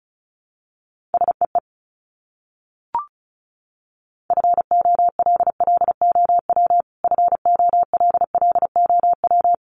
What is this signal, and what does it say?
Morse code, heard as a keyed tone, about 34 wpm, 700 Hz. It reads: SEE FOLLOW FOLLOW
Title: Top 500 Words Repeated 35wpm